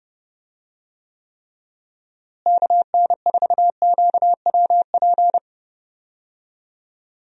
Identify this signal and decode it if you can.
Morse code, heard as a keyed tone, about 30 wpm, 700 Hz. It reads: KN4QWP
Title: USA Call Signs - 6 Characters - Repeated 3x - 30wpm